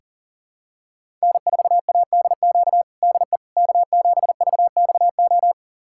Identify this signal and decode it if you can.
Morse code, heard as a keyed tone, about 40 wpm, 700 Hz. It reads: N4ADQ DE K7VXO
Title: USA Callsign DE Callsign - 5 Characters - 40wpm